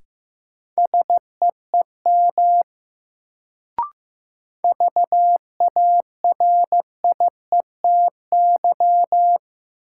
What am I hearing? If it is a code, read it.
Morse code, heard as a keyed tone, about 15 wpm, 700 Hz. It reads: SEEM VARIETY
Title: Sets of 3 Words - New Word Drill - 15wpm